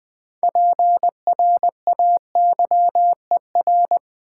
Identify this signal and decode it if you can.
Morse code, heard as a keyed tone, about 20 wpm, 700 Hz. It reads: PRAYER
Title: Build A Word - 3 Letters At A Time - 6-Letter Words - 20wpm